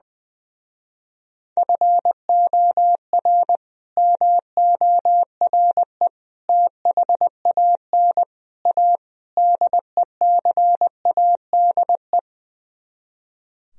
Morse code, heard as a keyed tone, about 21 wpm, 700 Hz. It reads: FOR MORE THAN A DECADE
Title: Speed-Racing - Sets of 5 Words 20wpm